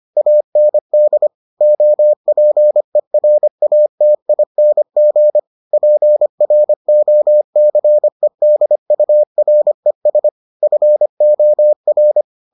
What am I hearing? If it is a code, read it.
Morse code, heard as a keyed tone, about 25 wpm, 600 Hz. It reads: AND OPERATING PROCEDURES FOR